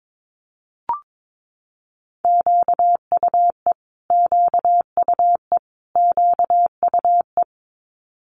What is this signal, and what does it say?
Morse code, heard as a keyed tone, about 22 wpm, 700 Hz. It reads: QUE QUE QUE
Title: Multi-letter Phonemes Repeated 3x 22wpm